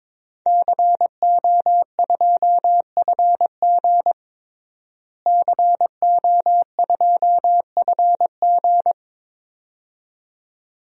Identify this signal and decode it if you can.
Morse code, heard as a keyed tone, about 23 wpm, 700 Hz. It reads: CO2FG CO2FG
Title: DXCC-100 Call Signs - 5 Characters - Repeated 2x - 22wpm